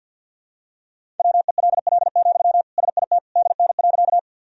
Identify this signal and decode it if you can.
Morse code, heard as a keyed tone, about 52 wpm, 700 Hz. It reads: WELL, SIT DN.
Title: Sentences from Top 500 Words and Abbreviations 50wpm